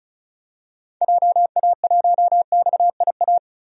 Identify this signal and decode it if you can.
Morse code, heard as a keyed tone, about 36 wpm, 700 Hz. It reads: JA1XIA
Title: DXCC-100 Call Signs - 6 Characters - Repeated 2x - 35wpm